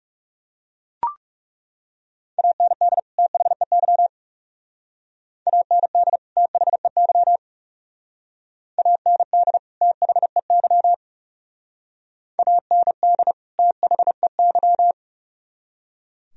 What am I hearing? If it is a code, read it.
Morse code, heard as a keyed tone, about 38 wpm, 700 Hz. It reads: AND THEY AND THEY AND THEY AND THEY